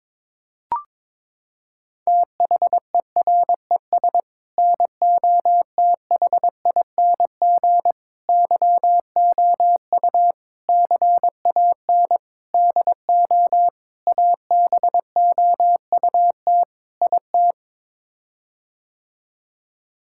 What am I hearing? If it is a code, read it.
Morse code, heard as a keyed tone, about 22 wpm, 700 Hz. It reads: THERES NOTHING YOU CAN DO ABOUT IT
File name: practice / 22wpm / Sets of 7 Words - 22wpm